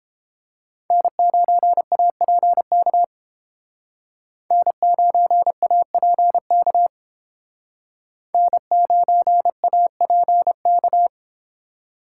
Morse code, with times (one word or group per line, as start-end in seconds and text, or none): 0.9-3.0 N9APK
4.5-6.9 N9APK
8.3-11.1 N9APK